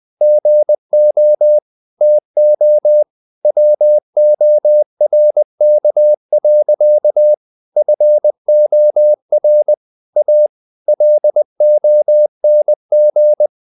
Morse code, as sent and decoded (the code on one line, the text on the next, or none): --. --- / - --- / .-- --- .-. -.- .-.-.- / ..-. --- .-. / .- / .-.. --- -. --.
GO TO WORK. FOR A LONG